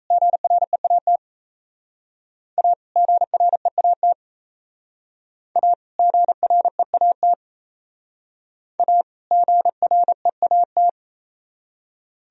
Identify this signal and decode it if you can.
Morse code, heard as a keyed tone, about 35 wpm, 700 Hz. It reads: GREAT A GREAT A GREAT A GREAT